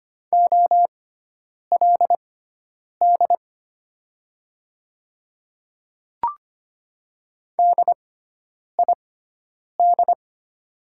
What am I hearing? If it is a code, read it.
Morse code, heard as a keyed tone, about 25 wpm, spaced as for 9 wpm, 700 Hz. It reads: OLD DID
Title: 6x Character Spacing - Top 200 Words - 25wpm